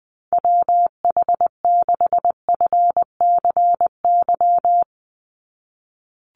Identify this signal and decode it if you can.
Morse code, heard as a keyed tone, about 20 wpm, 700 Hz. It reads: WH6FCY
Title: USA Call Signs 20wpm